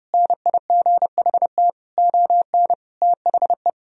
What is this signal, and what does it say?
Morse code, heard as a keyed tone, about 32 wpm, 700 Hz. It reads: NIGHT ON THE
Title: Sentences from Top 1000 Words - Review Part 2 - 30wpm